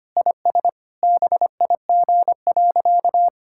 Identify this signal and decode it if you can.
Morse code, heard as a keyed tone, about 25 wpm, 700 Hz. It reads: IS BIG.
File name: practice / 25wpm / Sentences with 3-Word Phrases - 25wpm